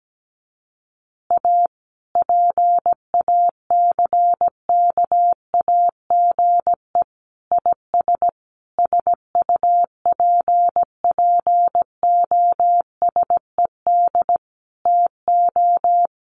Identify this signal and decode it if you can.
Morse code, heard as a keyed tone, about 17 wpm, 700 Hz. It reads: A PACKAGE IS SUPPOSED TO
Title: Easy Phrases 17wpm